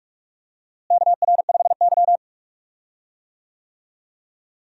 Morse code, heard as a keyed tone, about 46 wpm, 700 Hz. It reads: KR5Y
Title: USA Call Signs - 4 Characters - Repeated 2x - 45wpm